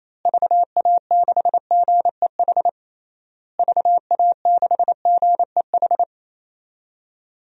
Morse code, heard as a keyed tone, about 28 wpm, 700 Hz. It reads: VA6GEH VA6GEH
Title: DXCC-100 Call Signs - 6 Characters - Repeated 3x - 28wpm